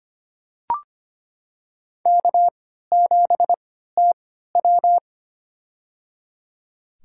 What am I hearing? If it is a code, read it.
Morse code, heard as a keyed tone, about 25 wpm, spaced as for 14 wpm, 700 Hz. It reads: K7TW
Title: USA Call Signs - 4 Characters - 3x Character Spacing - 25wpm